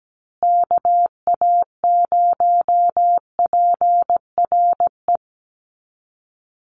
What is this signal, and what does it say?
Morse code, heard as a keyed tone, about 17 wpm, 700 Hz. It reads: KA0PRE
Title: USA Call Signs Repeated 3x 17wpm